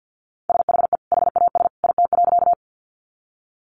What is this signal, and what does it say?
Morse code, heard as a keyed tone, about 50 wpm, 700 Hz. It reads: SHE HAS IT.
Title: Sentences from Top 100 Words 50wpm